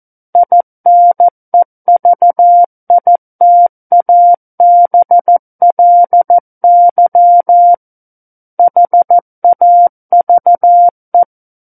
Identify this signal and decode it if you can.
Morse code, heard as a keyed tone, about 14 wpm, 700 Hz. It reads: INEVITABLY HAVE
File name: Book One / 05-Chapter 5 - 14 WPM